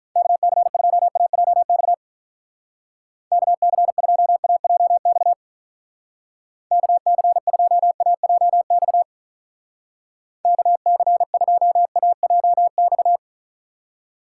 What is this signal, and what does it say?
Morse code, heard as a keyed tone, about 43 wpm, 700 Hz. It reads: KC2AJX KC2AJX KC2AJX KC2AJX
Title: Speed Racing - USA 2x3 Call Signs 35wpm